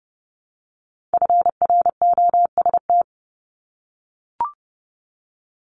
Morse code, heard as a keyed tone, about 31 wpm, 700 Hz. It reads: FROST
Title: Multi-letter Phonemes Repeated 3x 30wpm